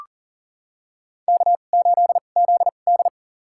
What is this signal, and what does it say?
Morse code, heard as a keyed tone, about 40 wpm, spaced as for 29 wpm, 700 Hz. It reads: K8ZD
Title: USA Call Signs - 4 Characters - 2x Character Spacing - 40wpm